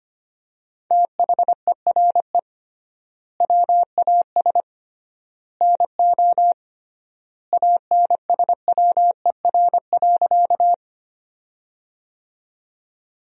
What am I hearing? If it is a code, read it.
Morse code, heard as a keyed tone, about 25 wpm, 700 Hz. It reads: THERE WAS NO ANSWER.
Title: 3x Word-Spacing - Sentences from Top 300 Words - 25wpm